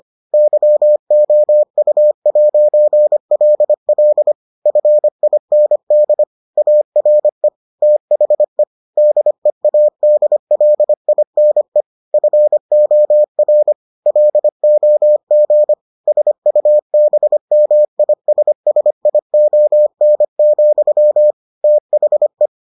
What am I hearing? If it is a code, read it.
Morse code, heard as a keyed tone, about 25 wpm, 600 Hz. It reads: YOU'LL FIND ARE THE DEADLINE FOR LOG SUBMISSION, THE